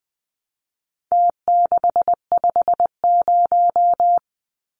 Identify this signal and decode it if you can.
Morse code, heard as a keyed tone, about 20 wpm, 700 Hz. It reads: T650
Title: Numbers - Contest Style - 4-Digit 20wpm